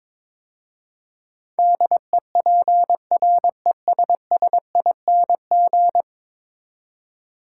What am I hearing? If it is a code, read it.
Morse code, heard as a keyed tone, about 22 wpm, 700 Hz. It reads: DEPRESSING
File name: practice / 22wpm / 10 Letter Words - First 3 Letters Given - 22wpm